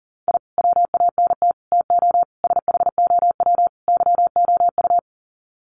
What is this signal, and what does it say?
Morse code, heard as a keyed tone, about 40 wpm, 700 Hz. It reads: I WANT TO SHOW YOU
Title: Sets of 5 Words 40wpm